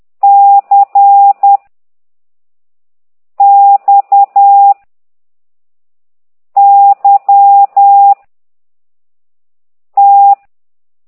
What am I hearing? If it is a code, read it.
Morse code, heard as a keyed tone, about 10 wpm, 800 Hz. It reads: C X Y T
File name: ct2idw / exerc9